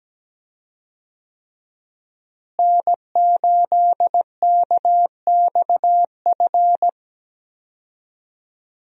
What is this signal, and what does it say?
Morse code, heard as a keyed tone, about 17 wpm, 700 Hz. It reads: N8KXF
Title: USA Call Signs - 5 Characters - Repeated 3x - 17wpm